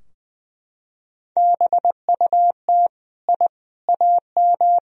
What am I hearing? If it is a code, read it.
Morse code, heard as a keyed tone, about 20 wpm, 700 Hz. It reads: BUT I AM
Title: Sentences from Top 1000 Words - Review Part 2 - 20wpm